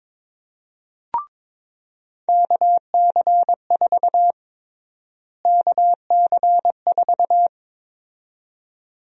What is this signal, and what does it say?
Morse code, heard as a keyed tone, about 21 wpm, 700 Hz. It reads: KC4 KC4
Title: USA Call Sign Prefixes - Repeated 2x - 22wpm